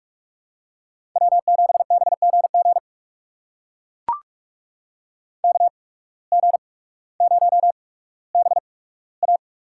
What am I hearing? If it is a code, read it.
Morse code, heard as a keyed tone, about 42 wpm, spaced as for 19 wpm, 700 Hz. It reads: W7BGG KG0BA
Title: USA Call Signs - 5 Characters - 8x Character Spacing - 45wpm